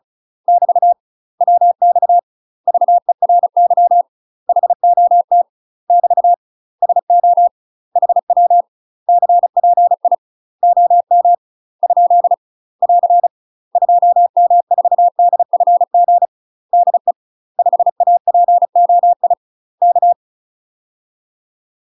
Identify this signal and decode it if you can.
Morse code, heard as a keyed tone, about 36 wpm, 700 Hz. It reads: <BT> WX VERY HOT <BT> SO HW CPI OM ? <AR> 2M4DFG DE 5APOI K
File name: UK MASTER QSOs TRAINING / 35WPM / UK MASTER QSOs 0028